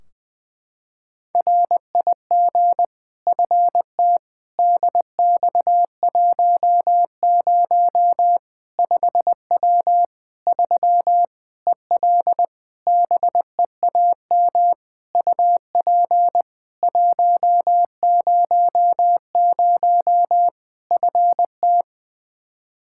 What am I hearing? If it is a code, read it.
Morse code, heard as a keyed tone, about 20 wpm, 700 Hz. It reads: RIG FT DX10 5W 3 EL BEAM UP 100 FT
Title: Practice QSO - Exchange 2 - Info - 20wpm